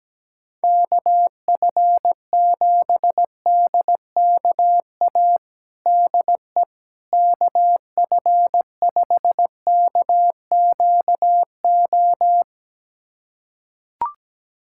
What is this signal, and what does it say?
Morse code, heard as a keyed tone, about 17 wpm, 700 Hz. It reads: KF7DKA DE KF5KQO
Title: USA Callsign DE Callsign - 6 Characters - 17wpm